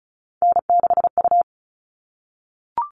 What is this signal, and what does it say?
Morse code, heard as a keyed tone, about 34 wpm, 700 Hz. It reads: N6U
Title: All Cut Numbers - 3-Digit 35wpm